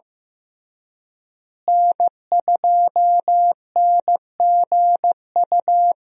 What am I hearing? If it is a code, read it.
Morse code, heard as a keyed tone, about 15 wpm, 700 Hz. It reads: N2NGU
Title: USA Call Signs - 5 Characters - Repeated 3x - 15wpm